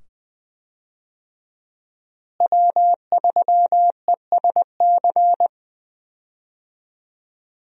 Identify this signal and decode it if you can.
Morse code, heard as a keyed tone, about 20 wpm, 700 Hz. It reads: W3ESC
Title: USA Call Signs - 5 Characters - Repeated 2x - 20wpm